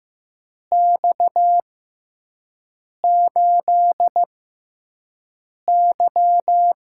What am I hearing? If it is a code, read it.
Morse code, heard as a keyed tone, about 15 wpm, spaced as for 5 wpm, 700 Hz. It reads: X8Y
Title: USA Call Signs - 4 Characters - 6x Character Spacing - 15wpm